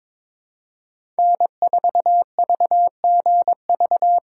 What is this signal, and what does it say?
Morse code, heard as a keyed tone, about 22 wpm, 700 Hz. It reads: N4VGV
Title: USA Call Signs - 5 Characters - 4x Character Spacing - 22wpm